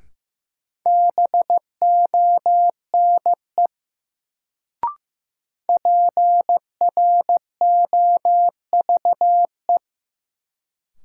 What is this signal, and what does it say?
Morse code, heard as a keyed tone, about 15 wpm, 700 Hz. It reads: BONE PROVE
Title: Sentences from Top 1600 Words - New Word Drill - 15wpm